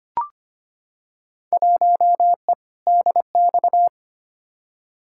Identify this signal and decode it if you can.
Morse code, heard as a keyed tone, about 25 wpm, 700 Hz. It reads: 1E DX
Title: Field Day Exchanges Encore - 25wpm